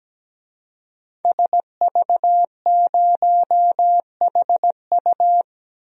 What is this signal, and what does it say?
Morse code, heard as a keyed tone, about 17 wpm, 700 Hz. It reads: SV0HU
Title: DXCC-100 Call Signs - 5 Characters - 17wpm